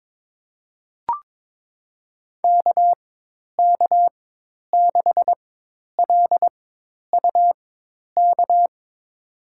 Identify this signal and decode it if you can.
Morse code, heard as a keyed tone, about 21 wpm, spaced as for 10 wpm, 700 Hz. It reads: KK6LUK
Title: USA Call Signs - 4x Character Spacing - 22wpm